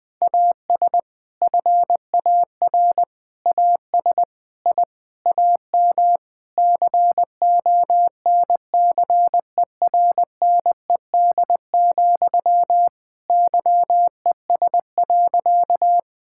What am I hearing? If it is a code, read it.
Morse code, heard as a keyed tone, about 20 wpm, 700 Hz. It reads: AS FAR AS I AM CONCERNED, YES.
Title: Sentences from Top 1000 Words - Review Part 6 - 20wpm